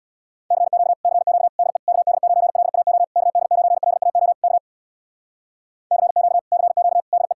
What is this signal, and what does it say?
Morse code, heard as a keyed tone, about 63 wpm, 700 Hz. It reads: CQ CQ DE KD9LAO KD9LAO K CQ CQ DE